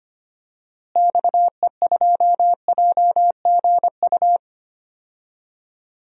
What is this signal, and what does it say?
Morse code, heard as a keyed tone, about 25 wpm, 700 Hz. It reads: XE2JGU
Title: DXCC-100 Call Signs - 6 Characters - Repeated 2x - 25wpm